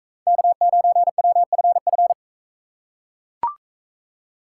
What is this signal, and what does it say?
Morse code, heard as a keyed tone, about 41 wpm, 700 Hz. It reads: K9WFF